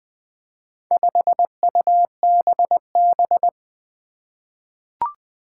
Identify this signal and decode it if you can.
Morse code, heard as a keyed tone, about 20 wpm, 700 Hz. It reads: 5UBB
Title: Four Challenging Letter-Number Combinations 20wpm